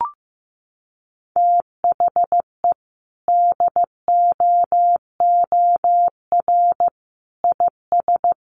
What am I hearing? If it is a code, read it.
Morse code, heard as a keyed tone, about 15 wpm, 700 Hz. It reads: THE DOOR IS
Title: Sentences from Top 1400 Words 15wpm